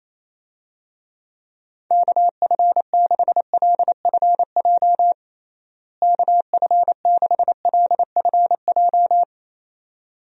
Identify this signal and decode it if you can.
Morse code, heard as a keyed tone, about 28 wpm, 700 Hz. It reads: KF6LFJ KF6LFJ
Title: USA Call Signs - 6 Characters - Repeated 2x - 28wpm